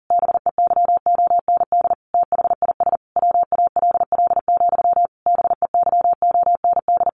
Morse code, heard as a keyed tone, about 40 wpm, 700 Hz. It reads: BEYOND THIS WALL, BEYOND
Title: A Sound Of Thunder 40wpm